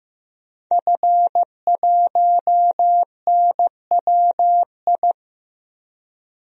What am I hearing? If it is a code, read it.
Morse code, heard as a keyed tone, about 15 wpm, 700 Hz. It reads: F1NWI
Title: DXCC-100 Call Signs - 5 Characters - 15wpm